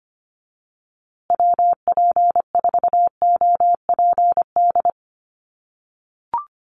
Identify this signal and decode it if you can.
Morse code, heard as a keyed tone, about 25 wpm, 700 Hz. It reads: WP4OPD
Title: USA Call Signs Repeated 2x 25wpm